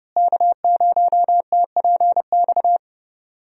Morse code, heard as a keyed tone, about 30 wpm, 700 Hz. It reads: K0TPX
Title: USA Call Signs - 5 Characters - Repeated 2x - 30wpm